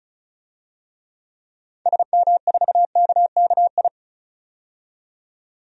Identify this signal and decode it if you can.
Morse code, heard as a keyed tone, about 36 wpm, 700 Hz. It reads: SM4KKI
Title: DXCC-100 Call Signs - 6 Characters - 35wpm